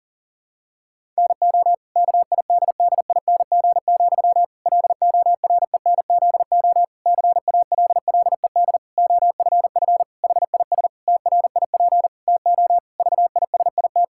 Morse code, heard as a keyed tone, about 40 wpm, 700 Hz. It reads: NO KIDDING, LORENZO CALLED OFF HIS TRIP TO VISIT
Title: Pangram Challenge v2 - 40wpm